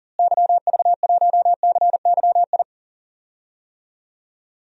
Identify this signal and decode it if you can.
Morse code, heard as a keyed tone, about 40 wpm, 700 Hz. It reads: YV1CYI
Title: DXCC-100 Call Signs - 6 Characters - Repeated 2x - 40wpm